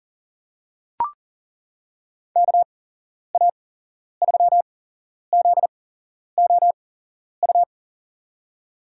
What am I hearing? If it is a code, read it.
Morse code, heard as a keyed tone, about 37 wpm, spaced as for 11 wpm, 690 Hz. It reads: KA3ZOU
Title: USA Call Signs - 8x Character Spacing - 40wpm